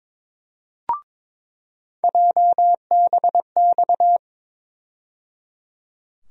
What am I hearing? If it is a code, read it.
Morse code, heard as a keyed tone, about 21 wpm, 700 Hz. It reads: JBX
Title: USA Call Sign Postfixes - 3 Letters 22wpm